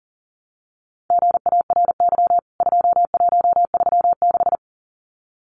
Morse code, heard as a keyed tone, about 40 wpm, 700 Hz. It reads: GARY 2136
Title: CWT Exchange 40wpm